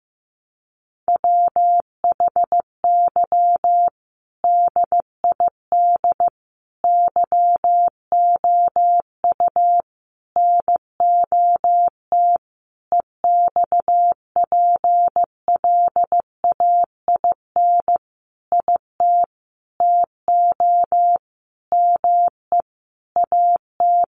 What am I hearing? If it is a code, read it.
Morse code, heard as a keyed tone, about 15 wpm, 700 Hz. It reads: WHY DID YOU NOT EXPLAIN IT TO ME AT
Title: Sentences from Top 1100 Words 15wpm